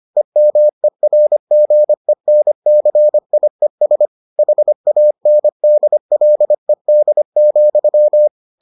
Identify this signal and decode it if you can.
Morse code, heard as a keyed tone, about 25 wpm, 600 Hz. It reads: EMERGENCIES HANDLED,